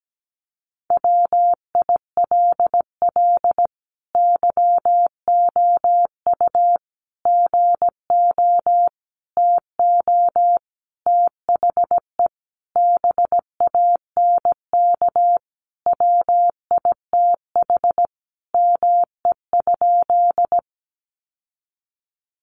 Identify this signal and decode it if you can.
Morse code, heard as a keyed tone, about 17 wpm, 700 Hz. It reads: WILL YOU GO TO THE BANK WITH ME?
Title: Sentences from Top 900 Words 17wpm